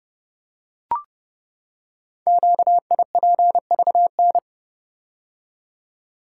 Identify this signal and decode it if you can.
Morse code, heard as a keyed tone, about 30 wpm, 700 Hz. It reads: QIPVN
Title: Code Groups - 5 Letter:Numbers - 30wpm